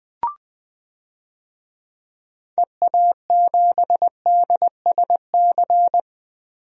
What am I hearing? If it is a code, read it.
Morse code, heard as a keyed tone, about 20 wpm, 700 Hz. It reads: EA7DSC
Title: DXCC-100 Call Signs - 6 Characters - 20wpm